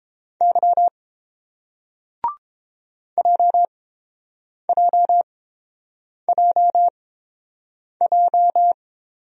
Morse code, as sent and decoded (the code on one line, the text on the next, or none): -.-- / .--- .--- .--- .---
Y JJJJ